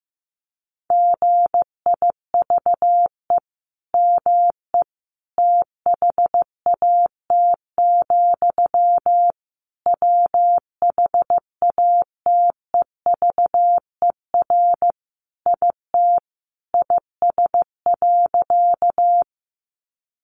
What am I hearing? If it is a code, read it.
Morse code, heard as a keyed tone, about 15 wpm, 700 Hz. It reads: GIVE ME THAT, WHATEVER IT IS.
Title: Sentences from Top 1100 Words 15wpm